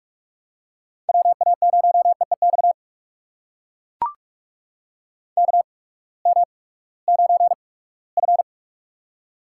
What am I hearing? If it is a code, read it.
Morse code, heard as a keyed tone, about 43 wpm, spaced as for 22 wpm, 700 Hz. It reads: WA0EEX KM9F
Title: USA Call Signs - 8x Character Spacing - 45wpm